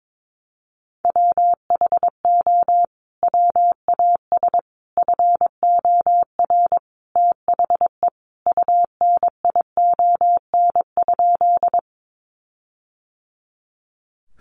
Morse code, heard as a keyed tone, about 22 wpm, 700 Hz. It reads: WHO WAS FOR THE UNION?
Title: Sentences from Top 1400 Words 22wpm